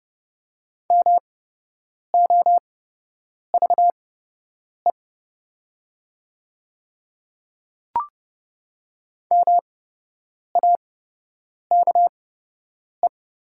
Morse code, heard as a keyed tone, about 30 wpm, spaced as for 8 wpm, 700 Hz. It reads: MOVE MAKE
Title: 8x Character Spacing - Top 200 Words - 30wpm